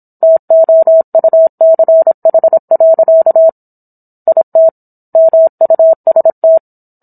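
Morse code, heard as a keyed tone, about 27 wpm, 650 Hz. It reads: TOUCH. IT MUST
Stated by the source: Morse code practice of a book chapter at 26 WPM.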